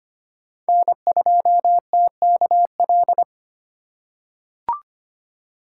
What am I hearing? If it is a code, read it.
Morse code, heard as a keyed tone, about 25 wpm, 700 Hz. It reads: N2TKL
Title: USA Call Signs - 5 Characters - 3x Character Spacing - 25wpm